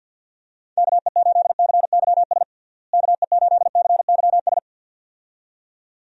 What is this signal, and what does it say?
Morse code, heard as a keyed tone, about 50 wpm, 700 Hz. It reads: KE8CYS KE8CYS
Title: USA Call Signs - 6 Characters - Repeated 2x - 50wpm